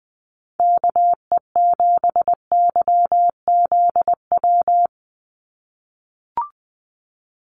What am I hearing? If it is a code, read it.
Morse code, heard as a keyed tone, about 20 wpm, 700 Hz. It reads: KE7YZW
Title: USA Call Signs 20wpm